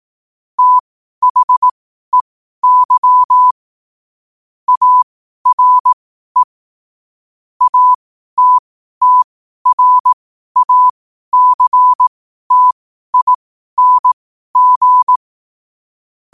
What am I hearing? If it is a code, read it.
Morse code, heard as a keyed tone, about 18 wpm, spaced as for 12 wpm, 1000 Hz. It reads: THEY ARE ATTRACTING
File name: qotdfiles / 12wpm